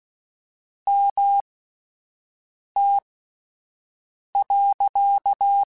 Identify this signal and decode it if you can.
Morse code, heard as a keyed tone, about 16 wpm, spaced as for 5 wpm, 800 Hz. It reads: MT.
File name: QSOs / QSOs-mp3s / WD0BGP 1@13